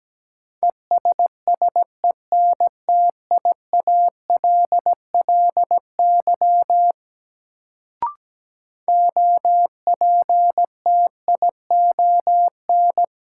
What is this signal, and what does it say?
Morse code, heard as a keyed tone, about 17 wpm, 700 Hz. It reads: ESSENTIALLY OPTION
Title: Sentences from Top 1300 Words - New Word Drill Alt - 17wpm